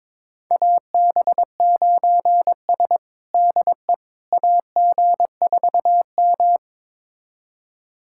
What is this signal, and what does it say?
Morse code, heard as a keyed tone, about 22 wpm, 700 Hz. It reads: AB9S DE AG4M
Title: USA Callsign DE Callsign - 4 Characters - 22wpm